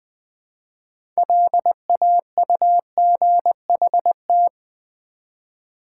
Morse code, heard as a keyed tone, about 20 wpm, 700 Hz. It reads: LAUGHT
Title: Build A Word - 3 Letters At A Time - 8-Letter Words - 20wpm